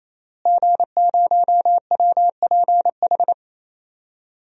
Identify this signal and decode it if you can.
Morse code, heard as a keyed tone, about 28 wpm, 700 Hz. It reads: G0WPH
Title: DXCC-100 Call Signs - 5 Characters - Repeated 3x - 28wpm